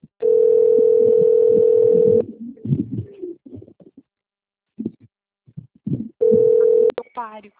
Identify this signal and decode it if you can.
Ringback tone (North America), 440 + 480 Hz, with speech.
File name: rg-1600-unknown- - - .29861